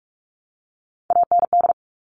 Morse code, heard as a keyed tone, about 44 wpm, 700 Hz. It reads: AND
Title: Top 300 Words 45wpm